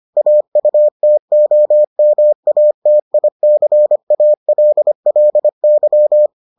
Morse code, heard as a keyed tone, about 25 wpm, 600 Hz. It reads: AUTOMATICALLY